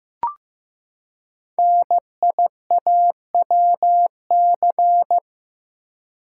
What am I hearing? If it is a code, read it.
Morse code, heard as a keyed tone, about 15 wpm, 700 Hz. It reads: NIAWC
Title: Code Groups - 5 Letters - 15wpm